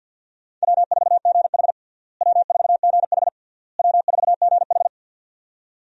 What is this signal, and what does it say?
Morse code, heard as a keyed tone, about 51 wpm, 700 Hz. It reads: W4GH W4GH W4GH
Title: USA Call Signs - 4 Characters - Repeated 3x - 50wpm